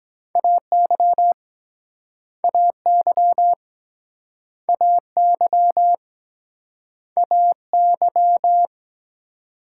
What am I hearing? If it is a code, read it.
Morse code, heard as a keyed tone, about 23 wpm, 700 Hz. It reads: AY AY AY AY